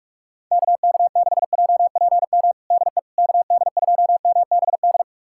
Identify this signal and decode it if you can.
Morse code, heard as a keyed tone, about 46 wpm, 700 Hz. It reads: KK6JPM DE KD2MBD
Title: USA Callsign DE Callsign - 6 Characters - 45wpm